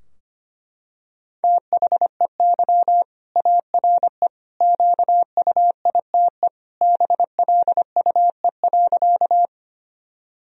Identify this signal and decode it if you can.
Morse code, heard as a keyed tone, about 25 wpm, 700 Hz. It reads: THEY ARE QUITE BLUE.
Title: Sentences from Top 1000 Words - Review Part 3 - 25wpm